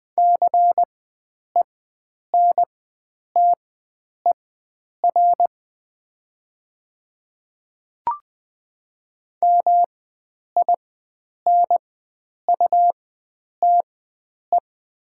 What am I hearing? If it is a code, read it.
Morse code, heard as a keyed tone, about 20 wpm, spaced as for 9 wpm, 700 Hz. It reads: CENTER MINUTE
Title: 4x Character Spacing - Top 500 Words - 20wpm